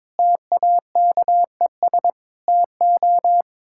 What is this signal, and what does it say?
Morse code, heard as a keyed tone, about 22 wpm, 700 Hz. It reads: TAKES TO
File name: practice / 22wpm / Sets of 6 Words - 22wpm